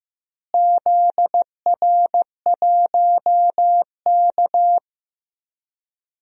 Morse code, heard as a keyed tone, about 15 wpm, 700 Hz. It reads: ZR1K
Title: DXCC-100 Call Signs - 4 Characters - Repeated 2x - 15wpm